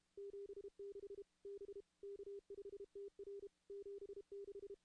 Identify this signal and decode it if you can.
Morse code, heard as a keyed tone, about 32 wpm, 400 Hz. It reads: 76 B K5TR 76
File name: contests / CW-SS-k5tr- -2100